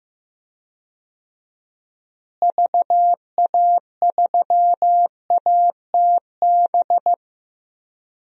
Morse code, heard as a keyed tone, about 15 wpm, 700 Hz. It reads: VA3ATB
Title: DXCC-100 Call Signs - 6 Characters - Repeated 3x - 15wpm